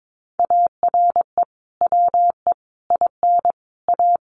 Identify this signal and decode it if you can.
Morse code, heard as a keyed tone, about 22 wpm, 700 Hz. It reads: ARE WE IN A
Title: Sentences from Top 1300 Words 22wpm